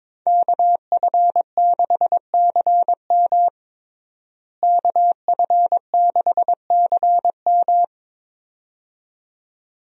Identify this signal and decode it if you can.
Morse code, heard as a keyed tone, about 22 wpm, 700 Hz. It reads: KF6CM KF6CM
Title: USA Call Signs - 5 Characters - Repeated 2x - 22wpm